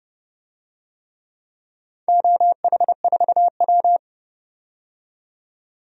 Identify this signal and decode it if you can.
Morse code, heard as a keyed tone, about 29 wpm, 700 Hz. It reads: OH4W
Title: DXCC-100 Call Signs - 4 Characters - 30wpm